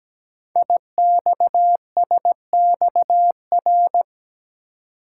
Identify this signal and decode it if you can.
Morse code, heard as a keyed tone, about 17 wpm, 700 Hz. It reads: IXSXR
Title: Code Groups - 5 Letter:Numbers - 17wpm